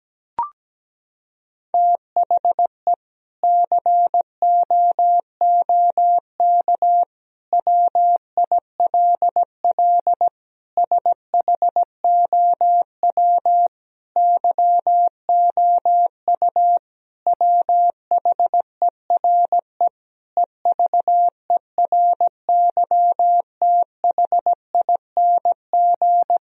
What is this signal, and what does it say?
Morse code, heard as a keyed tone, about 17 wpm, 700 Hz. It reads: THE COOK WILL SHOW YOU WHERE EVERYTHING
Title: Sentences from Top 1000 Words - Review Part 4 - 17wpm